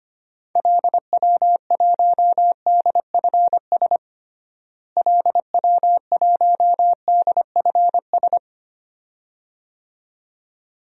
Morse code, heard as a keyed tone, about 25 wpm, 700 Hz. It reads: LW1DFS LW1DFS
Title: DXCC-100 Call Signs - 6 Characters - Repeated 2x - 25wpm